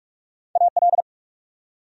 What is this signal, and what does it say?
Morse code, heard as a keyed tone, about 45 wpm, 700 Hz. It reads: AL
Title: USA Common Names - Max 3 Letters - 45wpm